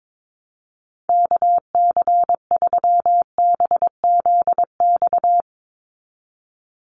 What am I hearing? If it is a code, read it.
Morse code, heard as a keyed tone, about 22 wpm, 700 Hz. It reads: KC3BZX
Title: USA Call Signs 22wpm